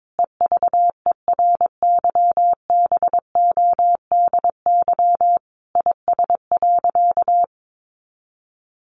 Morse code, heard as a keyed tone, about 22 wpm, 700 Hz. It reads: EVERYBODY IS.
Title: Sentences from Top 1400 Words 22wpm